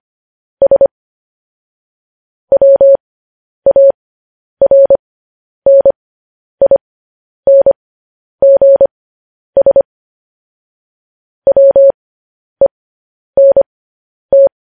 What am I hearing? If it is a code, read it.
Morse code, heard as a keyed tone, about 25 wpm, spaced as for 10 wpm, 550 Hz. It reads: S WARNINGS WENT